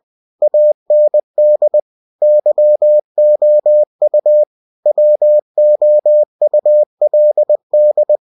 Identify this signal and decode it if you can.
Morse code, heard as a keyed tone, about 20 wpm, 600 Hz. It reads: AND YOU WOULD
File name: cw-books / winnie-the-pooh / winnie-20wpm-page-0036